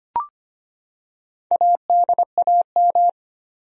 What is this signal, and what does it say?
Morse code, heard as a keyed tone, about 25 wpm, 700 Hz. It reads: ADAM
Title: USA Common Names - Max 4 Letters - 25wpm